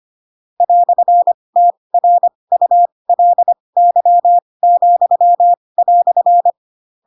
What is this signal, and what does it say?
Morse code, heard as a keyed tone, about 25 wpm, spaced as for 20 wpm, 700 Hz. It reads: "TRULY,"
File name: Pinocchio / Pinocchio ch8 15 25 ik1hge